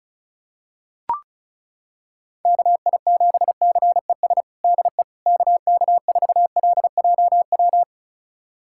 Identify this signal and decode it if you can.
Morse code, heard as a keyed tone, about 36 wpm, 700 Hz. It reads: KI7CES DE KK4LJW
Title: USA Callsign DE Callsign 35wpm